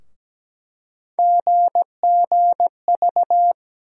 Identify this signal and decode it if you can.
Morse code, heard as a keyed tone, about 17 wpm, 700 Hz. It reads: GGV
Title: USA Call Sign Postfixes - 3 Letters 17wpm